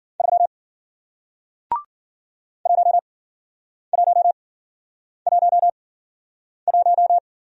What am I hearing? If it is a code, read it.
Morse code, heard as a keyed tone, about 54 wpm, spaced as for 9 wpm, 700 Hz. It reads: 3 1111